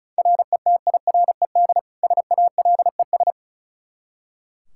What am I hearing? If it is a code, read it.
Morse code, heard as a keyed tone, about 36 wpm, 700 Hz. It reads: RETIRED SALES
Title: Practice QSO - Exchange 3 - More Info - 35wpm